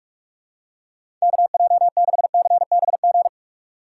Morse code, heard as a keyed tone, about 44 wpm, 700 Hz. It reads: KJ6CBG
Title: USA Call Signs - 6x Character Spacing - 45wpm